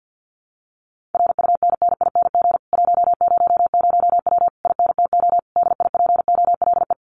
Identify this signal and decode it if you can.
Morse code, heard as a keyed tone, about 48 wpm, 700 Hz. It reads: RUNNING 100W INTO DIPOLE
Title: Common QSO Phrases Encore 50wpm